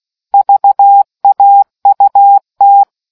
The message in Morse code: VAUT

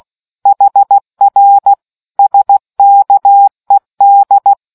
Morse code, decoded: HR SKED

